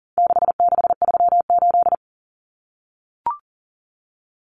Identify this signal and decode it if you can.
Morse code, heard as a keyed tone, about 40 wpm, 700 Hz. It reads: 6638
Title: Numbers - 4-Digit 40wpm